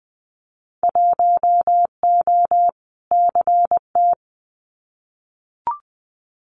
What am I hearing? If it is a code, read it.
Morse code, heard as a keyed tone, about 20 wpm, 700 Hz. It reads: 1O CT
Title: Winter Field Day Exchanges Encore 20wpm